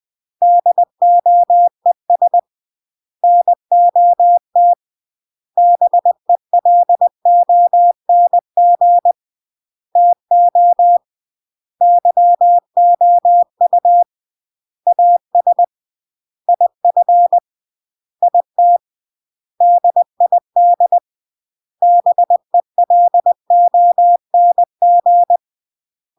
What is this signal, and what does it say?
Morse code, heard as a keyed tone, about 20 wpm, 700 Hz. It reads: DOES NOT BELONG TO YOU AS IF IT DID BELONG
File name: CW / jodorowsky / 82 commandments 20-20